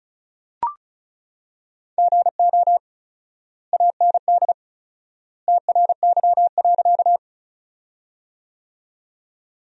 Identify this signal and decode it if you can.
Morse code, heard as a keyed tone, about 36 wpm, 700 Hz. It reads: GO AND TRY.
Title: 4x Word-Spacing - Sentences from Top 200 Words - 35wpm